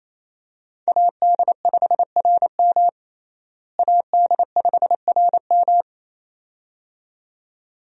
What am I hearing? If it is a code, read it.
Morse code, heard as a keyed tone, about 28 wpm, 700 Hz. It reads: AD5RM AD5RM
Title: USA Call Signs - 5 Characters - Repeated 2x - 28wpm